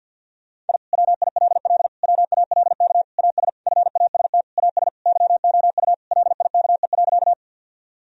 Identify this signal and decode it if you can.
Morse code, heard as a keyed tone, about 49 wpm, 700 Hz. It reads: I WILL WALK AS FAST AS YOU LIKE.
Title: Sentences from Top 1000 Words - Review Part 2 - 50wpm